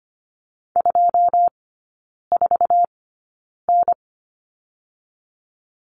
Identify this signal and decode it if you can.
Morse code, heard as a keyed tone, about 25 wpm, spaced as for 9 wpm, 690 Hz. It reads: 24N
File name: Lesson 14e - Number 2 - Three Characters 25wpm